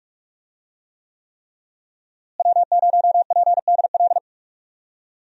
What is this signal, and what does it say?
Morse code, heard as a keyed tone, about 46 wpm, 700 Hz. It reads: W0PDL
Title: USA Call Signs - 5 Characters - Repeated 3x - 45wpm